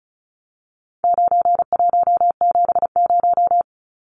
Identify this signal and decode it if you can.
Morse code, heard as a keyed tone, about 34 wpm, 700 Hz. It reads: 9170
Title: Numbers - 4-Digit 35wpm